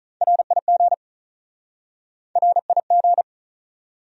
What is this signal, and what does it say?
Morse code, heard as a keyed tone, about 46 wpm, 700 Hz. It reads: RIG RIG